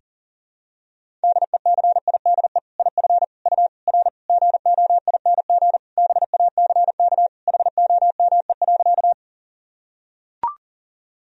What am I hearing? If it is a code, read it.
Morse code, heard as a keyed tone, about 40 wpm, 700 Hz. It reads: DECIDE IF U R GOING BACK HOME.
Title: Sentences from Top 500 Words and Abbreviations 40wpm